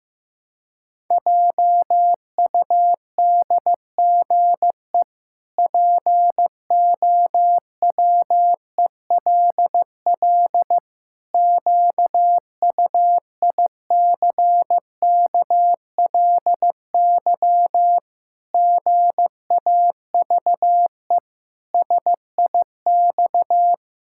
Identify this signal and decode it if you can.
Morse code, heard as a keyed tone, about 14 wpm, 700 Hz. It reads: JUDGE POWELL QUICKLY GAVE SIX
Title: Pangram Challenge v2 - 15wpm